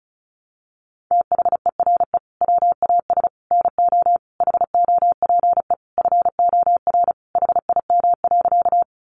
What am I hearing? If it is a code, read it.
Morse code, heard as a keyed tone, about 34 wpm, 700 Hz. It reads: THERE WAS NO HOPE FOR HIM.
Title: Sentences from Top 1000 Words 35wpm